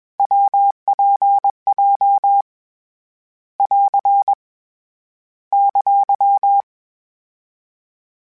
Morse code, heard as a keyed tone, about 21 wpm, 800 Hz. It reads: WPJ <AR> !
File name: Variable Speed / Variable Speed-mp3s / WPJ WPJ1 sp @20